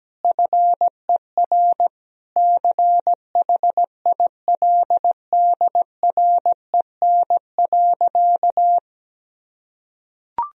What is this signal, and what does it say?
Morse code, heard as a keyed tone, about 17 wpm, 700 Hz. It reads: FER CHILDREN.
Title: Sentences from Top 400 Words and Abbreviations 17wpm